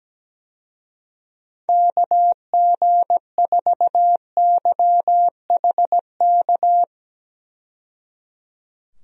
Morse code, heard as a keyed tone, about 17 wpm, 700 Hz. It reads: KG4YHK
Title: USA Call Signs - 6 Characters - 17wpm